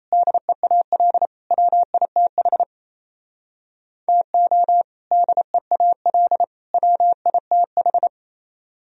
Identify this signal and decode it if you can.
Morse code, heard as a keyed tone, about 35 wpm, 700 Hz. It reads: DEAL WITH TO DEAL WITH